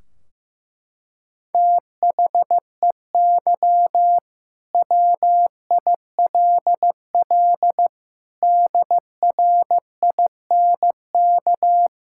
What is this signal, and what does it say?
Morse code, heard as a keyed tone, about 15 wpm, 700 Hz. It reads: THEY WILL DRINK
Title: Sentences from Top 1600 Words 15wpm